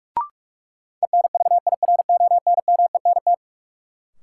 Morse code, heard as a keyed tone, about 46 wpm, 700 Hz. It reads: ENVIRONMENT
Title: Sentences from Top 1000 Words - Review Part 5 - New Word Drill - 45wpm